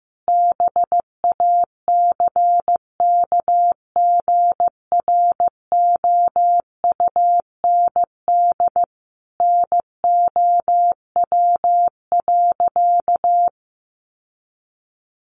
Morse code, heard as a keyed tone, about 15 wpm, 700 Hz. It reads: BACKGROUND NOW.
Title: Sentences from Top 1300 Words 15wpm